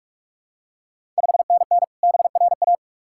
Morse code, heard as a keyed tone, about 44 wpm, 690 Hz. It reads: 5NN BRA
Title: Ontario QSO Party Exchange 45wpm